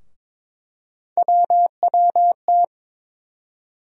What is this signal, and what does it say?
Morse code, heard as a keyed tone, about 21 wpm, 700 Hz. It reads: WWT
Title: USA Call Sign Postfixes - 3 Letters 22wpm